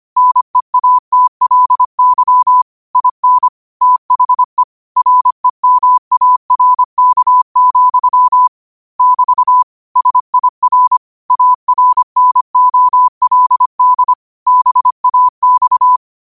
Morse code, heard as a keyed tone, about 25 wpm, 1000 Hz. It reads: NEATLY IN THE REMARK, <BT> SIR ARNOLD BAX
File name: qotdfiles / 25wpm